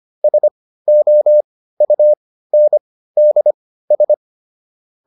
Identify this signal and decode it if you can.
Morse code, heard as a keyed tone, about 25 wpm, spaced as for 15 wpm, 600 Hz. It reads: SOUNDS
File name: SS203 15